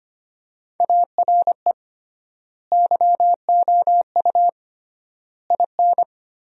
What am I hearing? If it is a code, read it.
Morse code, heard as a keyed tone, about 25 wpm, 700 Hz. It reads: ARE YOU IN
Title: 3x Word-Spacing - Sentences from Top 400 Words - 25wpm